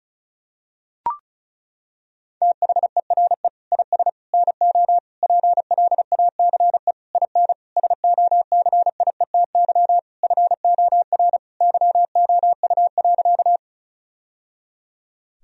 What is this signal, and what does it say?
Morse code, heard as a keyed tone, about 36 wpm, 700 Hz. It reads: THERE IS NO PLACE IN SOCIETY FOR YOU.
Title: Sentences from Top 1000 Words - Review Part 4 - 35wpm